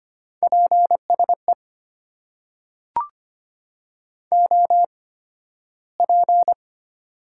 Morse code, heard as a keyed tone, about 25 wpm, spaced as for 13 wpm, 700 Hz. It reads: PSE OP
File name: Top 90 QSO Elements - 8x Character Spacing - 25wpm